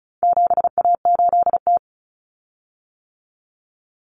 Morse code, heard as a keyed tone, about 34 wpm, 700 Hz. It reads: 7A8T
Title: Common Cut Numbers - 4-Digit 35wpm